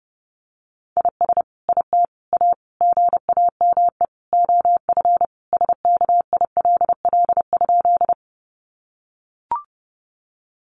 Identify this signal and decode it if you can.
Morse code, heard as a keyed tone, about 30 wpm, 700 Hz. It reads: IS IT A GAME OF SKILL?
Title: Sentences from Top 1200 Words 30wpm